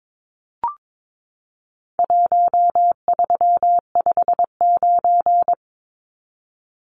Morse code, heard as a keyed tone, about 22 wpm, 700 Hz. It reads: 1359
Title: Numbers - Contest Style - 4-Digit 22wpm